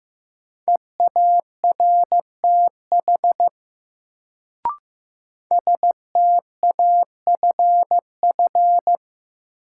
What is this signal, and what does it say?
Morse code, heard as a keyed tone, about 15 wpm, 700 Hz. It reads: EARTH STAFF
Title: Sets of 5 Words Encore - New Word Drill - 15wpm